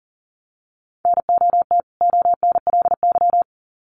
Morse code, heard as a keyed tone, about 40 wpm, 700 Hz. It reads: NOT ONLY
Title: Sets of 2 Words 40wpm